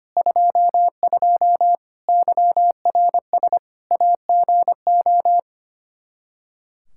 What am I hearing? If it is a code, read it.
Morse code, heard as a keyed tone, about 25 wpm, 700 Hz. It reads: 22 YRS AGO
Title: Practice QSO - Exchange 3 - More Info - 25wpm